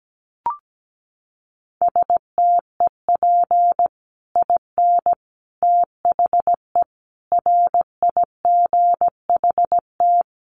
Morse code, heard as a keyed tone, about 17 wpm, 700 Hz. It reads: STEP IN THE RIGHT
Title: Sets of 5 Words Encore 17wpm